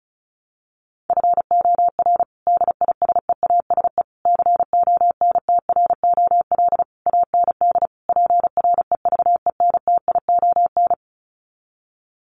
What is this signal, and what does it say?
Morse code, heard as a keyed tone, about 34 wpm, 700 Hz. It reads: FOR DISEASE CONTROL AND PREVENTION
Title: Sets of 5 Words Encore2 35wpm